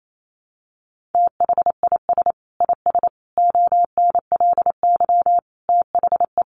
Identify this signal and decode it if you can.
Morse code, heard as a keyed tone, about 28 wpm, 700 Hz. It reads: THIS IS ONLY THE
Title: Sentences from Top 800 Words 28wpm